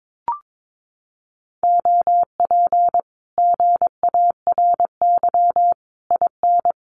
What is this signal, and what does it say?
Morse code, heard as a keyed tone, about 22 wpm, 700 Hz. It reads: OP GARY IN
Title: Common QSO Phrases Encore 22wpm